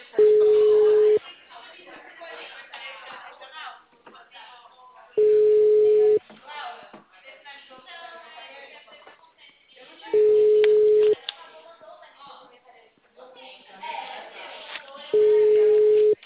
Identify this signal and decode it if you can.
Ringback tone (Europe, 425 Hz), with speech.